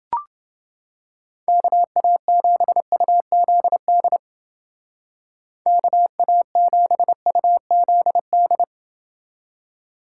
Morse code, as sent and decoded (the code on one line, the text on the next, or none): -.- .- --... ..- --.. -.. / -.- .- --... ..- --.. -..
KA7UZD KA7UZD